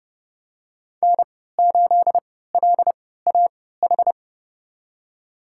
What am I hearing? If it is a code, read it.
Morse code, heard as a keyed tone, about 30 wpm, spaced as for 17 wpm, 700 Hz. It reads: N8LAH
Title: USA Call Signs - 5 Characters - 3x Character Spacing - 30wpm